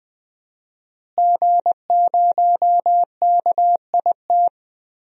Morse code, heard as a keyed tone, about 20 wpm, 700 Hz. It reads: G0KIT
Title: DXCC-100 Call Signs - 5 Characters - Repeated 2x - 20wpm